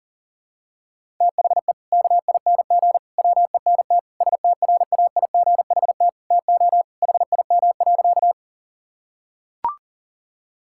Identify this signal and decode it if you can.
Morse code, heard as a keyed tone, about 40 wpm, 700 Hz. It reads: THE KING WENT STRAIGHT TO HIM.
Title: Sentences from Top 1000 Words - Review Part 6 - 40wpm